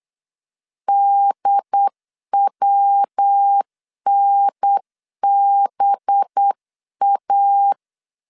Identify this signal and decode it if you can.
Morse code, heard as a keyed tone, about 8 wpm, 800 Hz. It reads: DWNBA
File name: CW / DET / 1 9 1500C S A 8wpm 5637